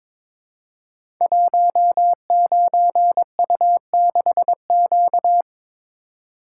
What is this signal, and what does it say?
Morse code, heard as a keyed tone, about 22 wpm, 700 Hz. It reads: 19U6Q
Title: Code Groups - 5 Letter:Numbers - 22wpm